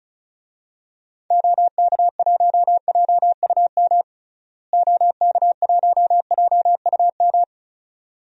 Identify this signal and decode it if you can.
Morse code, heard as a keyed tone, about 36 wpm, 700 Hz. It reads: OK1JUM OK1JUM
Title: DXCC-100 Call Signs - 6 Characters - Repeated 2x - 35wpm